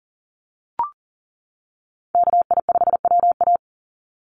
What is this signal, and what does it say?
Morse code, heard as a keyed tone, about 39 wpm, 700 Hz. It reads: KI5WA
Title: NA SOTA Chaser Callsigns 40wpm